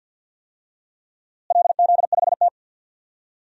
Morse code, heard as a keyed tone, about 45 wpm, 700 Hz. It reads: LZ5T
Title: DXCC-100 Call Signs - 4 Characters - Repeated 3x - 50wpm